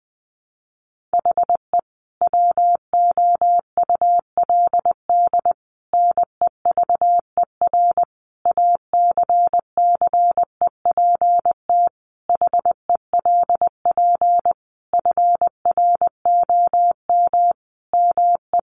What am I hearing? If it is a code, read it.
Morse code, heard as a keyed tone, about 19 wpm, 700 Hz. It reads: HE WOULD NEVER ACCEPT HELP FROM ME
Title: Sentences from Top 1300 Words 20wpm